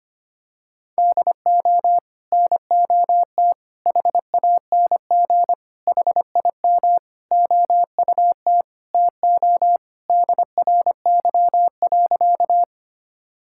Morse code, heard as a keyed tone, about 25 wpm, 700 Hz. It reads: DO NOT HANG HIM OUT TO DRY.
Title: Sentences from Top 1600 Words 25wpm